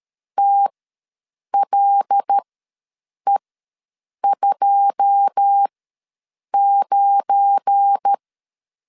Morse code, heard as a keyed tone, about 13 wpm, spaced as for 7 wpm, 800 Hz. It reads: TLE29
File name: LEZIONI / 10 SIGNS / 10C DOT ws 12wpm 3548